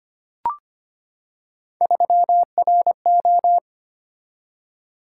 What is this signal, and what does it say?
Morse code, heard as a keyed tone, about 25 wpm, 700 Hz. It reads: 3RO
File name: Three Letter-Number Combinations - 25wpm